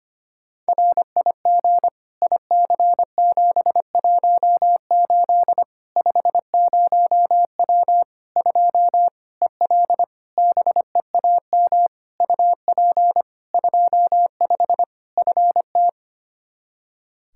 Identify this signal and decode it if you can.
Morse code, heard as a keyed tone, about 25 wpm, 700 Hz. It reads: RIG IC718 50W 2 EL BEAM UP 25 FT
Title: K1USN SST QSO Practice 25wpm